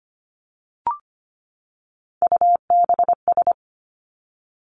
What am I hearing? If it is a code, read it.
Morse code, heard as a keyed tone, about 25 wpm, 700 Hz. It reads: UBS
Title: Three Challenging Letter-Number Combinations 25wpm